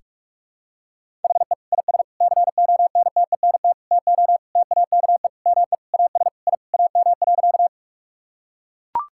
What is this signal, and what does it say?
Morse code, heard as a keyed tone, about 42 wpm, 700 Hz. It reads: HE IS CONTENT TO TAKE ME AS I AM.
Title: Sentences from Top 1000 Words - Review Part 6 - 45wpm